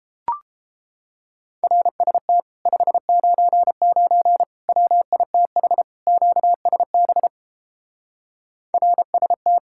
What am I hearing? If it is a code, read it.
Morse code, heard as a keyed tone, about 31 wpm, 700 Hz. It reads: RST 599 WITH QSB RST